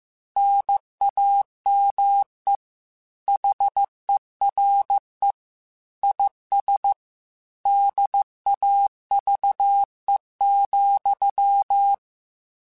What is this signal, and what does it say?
Morse code, heard as a keyed tone, about 15 wpm, 800 Hz. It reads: NAME HERE IS DAVE,
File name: QSOs / QSOs-mp3s / WD0BGP 1@10